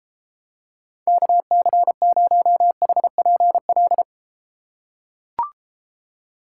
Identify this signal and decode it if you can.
Morse code, heard as a keyed tone, about 31 wpm, 700 Hz. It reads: KC0HPL